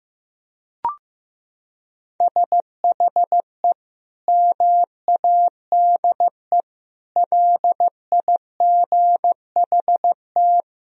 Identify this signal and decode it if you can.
Morse code, heard as a keyed tone, about 15 wpm, 700 Hz. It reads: SHE MADE LIGHT
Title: Sentences from Top 1000 Words - Review Part 1 - 15wpm